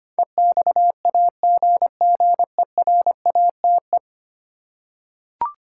Morse code, heard as a keyed tone, about 25 wpm, 700 Hz. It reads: EXAGGERATE
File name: 10 Letter Words - First 3 Letters Given - 25wpm